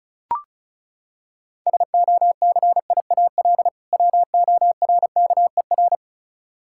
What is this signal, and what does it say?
Morse code, heard as a keyed tone, about 36 wpm, 700 Hz. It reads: SOCIAL WORKER
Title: Practice QSO - Exchange 3 - More Info - 35wpm